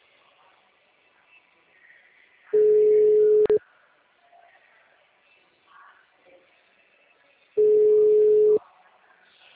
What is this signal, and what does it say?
Ringback tone (Europe, 425 Hz).
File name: out- -2040- - - .58